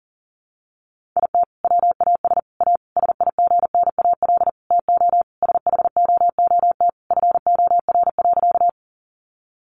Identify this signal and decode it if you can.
Morse code, heard as a keyed tone, about 40 wpm, 700 Hz. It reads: IT WAS A SIGNAL TO SHOOT FOR.
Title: Sentences from Top 1400 Words 40wpm